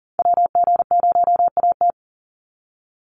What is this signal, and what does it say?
Morse code, heard as a keyed tone, about 40 wpm, 700 Hz. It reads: WG0AT
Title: NA SOTA Activator Callsigns 40wpm